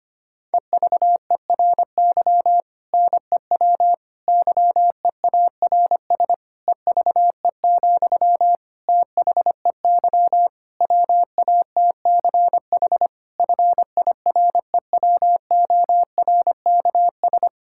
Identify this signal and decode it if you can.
Morse code, heard as a keyed tone, about 25 wpm, 700 Hz. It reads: EVERY NEW YEARS EVE, THEY WATCH FIREWORKS
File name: practice / 25wpm / Difficult Sentences with Hints - 25wpm